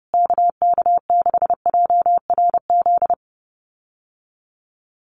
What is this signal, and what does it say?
Morse code, heard as a keyed tone, about 30 wpm, 700 Hz. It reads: KK6JRZ
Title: USA Call Signs Repeated 2x 30wpm